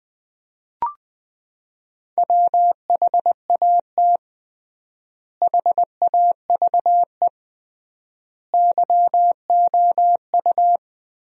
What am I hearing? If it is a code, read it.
Morse code, heard as a keyed tone, about 20 wpm, 700 Hz. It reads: WHAT HAVE YOU
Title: 3x Word-Spacing - Sentences from Top 100 Words - 20wpm